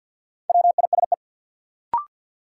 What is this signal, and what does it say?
Morse code, heard as a keyed tone, about 52 wpm, 700 Hz. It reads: WISE
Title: Sentences from Top 1600 Words - New Word Drill - 50wpm